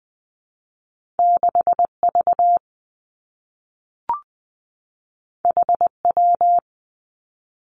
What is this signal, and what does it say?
Morse code, heard as a keyed tone, about 20 wpm, 700 Hz. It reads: 6 V H W
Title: Two Challenging Letter-Number Combinations 20wpm